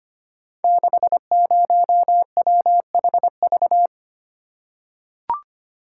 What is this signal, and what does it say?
Morse code, heard as a keyed tone, about 25 wpm, 700 Hz. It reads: 60WHV
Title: Code Groups - 5 Letter:Numbers - 25wpm